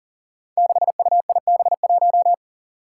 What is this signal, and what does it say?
Morse code, heard as a keyed tone, about 39 wpm, 700 Hz. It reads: 6UIB1
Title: Five Challenging Letter-Number Combinations 40wpm